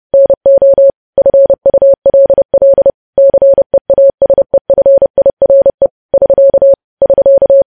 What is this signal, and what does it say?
Morse code, heard as a keyed tone, about 30 wpm, 550 Hz. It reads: NO FULL CEASEFIRE <SK> <SK>
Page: news headlines in morse code at 30 wpm